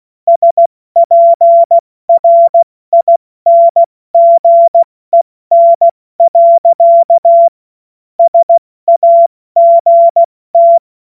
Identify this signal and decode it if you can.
Morse code, heard as a keyed tone, about 16 wpm, 680 Hz. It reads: SPRINGEN. SAGT
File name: Morsekurs / Witze / Witz 16. Februar 2021, 70 F80